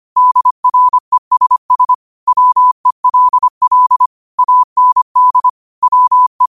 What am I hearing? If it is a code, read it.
Morse code, heard as a keyed tone, about 25 wpm, 1000 Hz. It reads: DRESS WELL AND WE
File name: qotdfiles / 25wpm